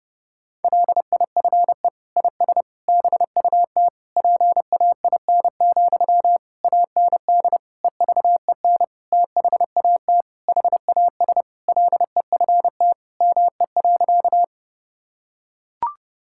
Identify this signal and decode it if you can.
Morse code, heard as a keyed tone, about 32 wpm, 700 Hz. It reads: LIFE IS BUT PAIN, AND EVEN THAT HAS LEFT ME.
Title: Sentences from Top 1600 Words 30wpm